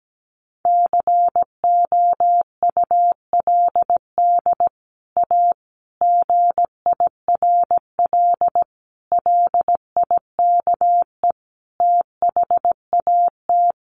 Morse code, heard as a keyed tone, about 17 wpm, 700 Hz. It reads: COULD A GIRL LIKE THAT
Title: Sentences from Top 1200 Words 17wpm